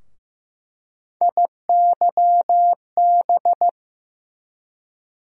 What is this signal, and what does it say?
Morse code, heard as a keyed tone, about 15 wpm, 700 Hz. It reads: IYB
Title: USA Call Sign Postfixes - 3 Letters 15wpm